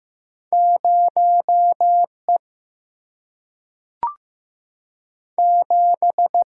Morse code, heard as a keyed tone, about 15 wpm, 700 Hz. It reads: 0E 7
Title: Single-Letter Single-Number 15wpm